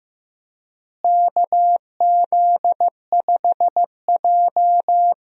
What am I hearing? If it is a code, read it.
Morse code, heard as a keyed tone, about 15 wpm, 700 Hz. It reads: KZ5J
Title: USA Call Signs - 4 Characters - Repeated 2x - 15wpm